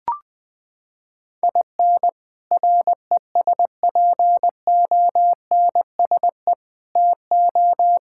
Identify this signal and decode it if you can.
Morse code, heard as a keyed tone, about 20 wpm, 700 Hz. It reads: IN RESPONSE TO
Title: Sets of 6 Words - 20wpm